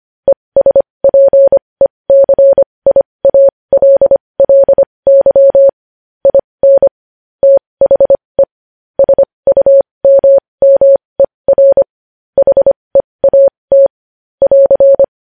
Morse code, heard as a keyed tone, about 25 wpm, spaced as for 20 wpm, 550 Hz. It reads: ESPECIALLY IN THE SUMMER HEAT <AR>
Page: news-headlines-in-morse-code-at-20-wpm